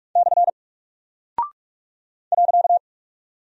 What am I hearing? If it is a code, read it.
Morse code, heard as a keyed tone, about 44 wpm, 700 Hz. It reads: / .
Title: Single Character (Letter:Number:Punctuation:Prosign) 45wpm